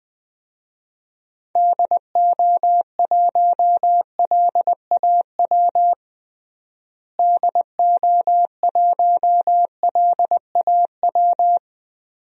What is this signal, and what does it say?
Morse code, heard as a keyed tone, about 19 wpm, 700 Hz. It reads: DO1LAW DO1LAW
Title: DXCC-100 Call Signs - 6 Characters - Repeated 2x - 20wpm